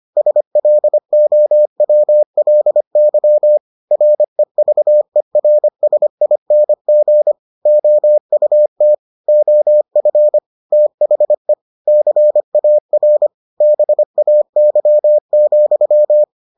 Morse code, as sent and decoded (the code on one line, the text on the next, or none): ... .-.. --- .-- .-.. -.-- / .-. ...- . .-. ... .. -. --. / --- ..- - / --- ..-. / - .... / -.-. .- .-. / -... .- -.-- --..--
SLOWLY REVERSING OUT OF THE CAR BAY,